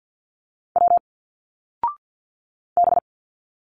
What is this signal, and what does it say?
Morse code, heard as a keyed tone, about 50 wpm, 700 Hz. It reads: W B
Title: Single Letters 50wpm